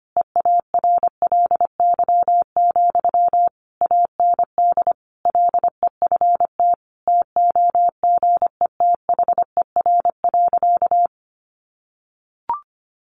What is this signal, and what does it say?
Morse code, heard as a keyed tone, about 25 wpm, 700 Hz. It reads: EARLY, AND LEFT TOGETHER.
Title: Sentences from Top 400 Words 25wpm